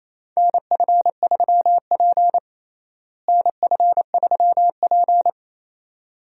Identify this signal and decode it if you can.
Morse code, heard as a keyed tone, about 28 wpm, 700 Hz. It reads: NF3P NF3P
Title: USA Call Signs - 4 Characters - Repeated 2x - 28wpm